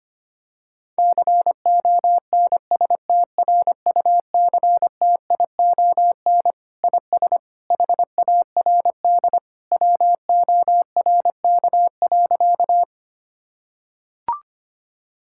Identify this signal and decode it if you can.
Morse code, heard as a keyed tone, about 25 wpm, 700 Hz. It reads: CONSTRUCTION IS HARD WORK.
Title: Sentences from Top 1600 Words 25wpm